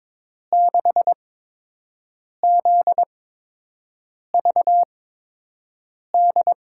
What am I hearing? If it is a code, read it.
Morse code, heard as a keyed tone, about 22 wpm, spaced as for 6 wpm, 700 Hz. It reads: 6ZVD
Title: USA Call Signs - 8x Character Spacing - 22wpm